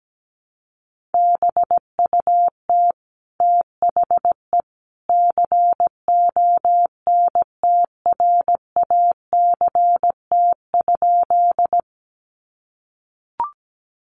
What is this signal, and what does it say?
Morse code, heard as a keyed tone, about 17 wpm, 700 Hz. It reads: BUT THE CONTRACT?
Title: Sentences from Top 1400 Words 17wpm